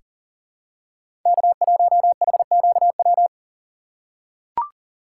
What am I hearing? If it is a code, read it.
Morse code, heard as a keyed tone, about 40 wpm, 700 Hz. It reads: K1HQW
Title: USA Call Signs - 5 Characters - 2x Character Spacing - 40wpm